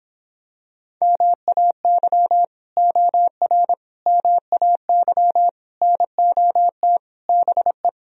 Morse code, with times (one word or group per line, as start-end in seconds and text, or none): 1.0-2.4 MAY
2.8-3.7 OR
4.1-5.5 MAY
5.8-7.0 NOT
7.3-7.9 BE